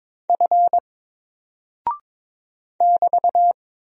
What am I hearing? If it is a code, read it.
Morse code, heard as a keyed tone, about 22 wpm, 700 Hz. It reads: F <BT>
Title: Single Character (Letter:Number:Punctuation:Prosign) 22wpm